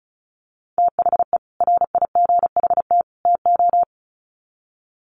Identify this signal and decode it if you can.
Morse code, heard as a keyed tone, about 34 wpm, 700 Hz. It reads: THE RIGHT TO
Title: Sets of 3 Words 35wpm